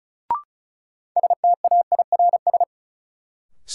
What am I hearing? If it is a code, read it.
Morse code, heard as a keyed tone, about 36 wpm, 700 Hz. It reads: STAIRS
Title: Sets of 5 Words Encore - New Word Drill - 35wpm